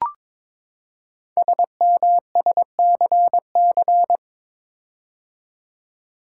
Morse code, heard as a keyed tone, about 22 wpm, 700 Hz. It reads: SMSCC
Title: Code Groups - 5 Letters - 22wpm